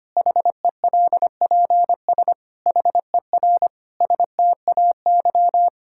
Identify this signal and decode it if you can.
Morse code, heard as a keyed tone, about 25 wpm, 700 Hz. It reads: HELPS HER STAY
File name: practice / 25wpm / Difficult Sentences with Hints - 25wpm